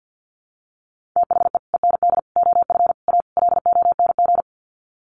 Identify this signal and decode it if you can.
Morse code, heard as a keyed tone, about 50 wpm, 700 Hz. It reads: THE END OF A LONG
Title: Sets of 5 Words Encore2 50wpm